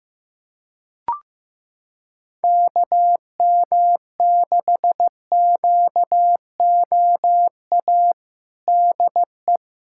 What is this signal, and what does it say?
Morse code, heard as a keyed tone, about 15 wpm, 700 Hz. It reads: KM6QOA DE
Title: USA Callsign DE Callsign - 6 Characters - 15wpm